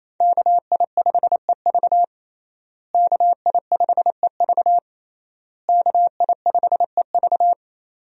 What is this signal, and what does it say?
Morse code, heard as a keyed tone, about 28 wpm, 700 Hz. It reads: KI5EV KI5EV KI5EV
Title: USA Call Signs - 5 Characters - Repeated 3x - 28wpm